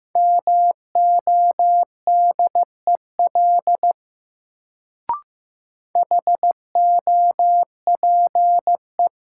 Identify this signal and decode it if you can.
Morse code, heard as a keyed tone, about 15 wpm, 700 Hz. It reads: MODEL HOPE
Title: Sentences from Top 1000 Words - New Word Drill - 15wpm